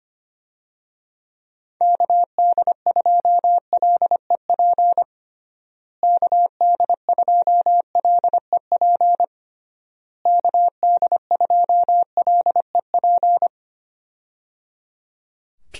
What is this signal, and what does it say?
Morse code, heard as a keyed tone, about 25 wpm, 700 Hz. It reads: KD2LEP KD2LEP KD2LEP
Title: USA Call Signs - 6 Characters - Repeated 3x - 25wpm